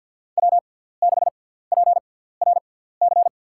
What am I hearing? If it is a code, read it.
Morse code, heard as a keyed tone, about 53 wpm, spaced as for 17 wpm, 700 Hz. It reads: W6PRC
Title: USA Call Signs - 6x Character Spacing - 50wpm